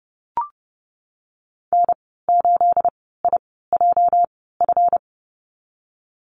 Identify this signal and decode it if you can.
Morse code, heard as a keyed tone, about 30 wpm, spaced as for 17 wpm, 700 Hz. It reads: N8IJF
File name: Lesson 37g - Letter x - Call Signs 30wpm